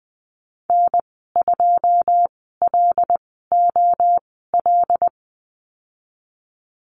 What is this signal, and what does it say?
Morse code, heard as a keyed tone, about 19 wpm, spaced as for 14 wpm, 700 Hz. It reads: N2LOL
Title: 082 Lesson 14g - Number 2 - Call Signs 20wpm